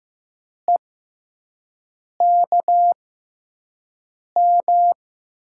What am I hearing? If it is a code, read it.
Morse code, heard as a keyed tone, about 15 wpm, spaced as for 5 wpm, 700 Hz. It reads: EKM
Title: USA Call Signs - 6 Characters - 6x Character Spacing - 15wpm